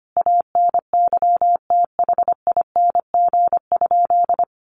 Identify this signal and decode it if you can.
Morse code, heard as a keyed tone, about 25 wpm, 700 Hz. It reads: ANYTHING?
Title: Sentences from Top 1100 Words 25wpm